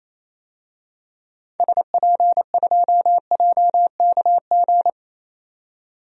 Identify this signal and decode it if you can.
Morse code, heard as a keyed tone, about 28 wpm, 700 Hz. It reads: SP2JKG
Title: DXCC-100 Call Signs - 6 Characters - 28wpm